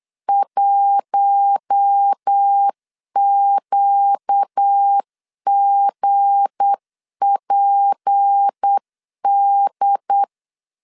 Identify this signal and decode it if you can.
Morse code, heard as a keyed tone, about 8 wpm, 800 Hz. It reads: 1QGPD
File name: LEZIONI / 7 GF9Z / 7 G 8wpm 3118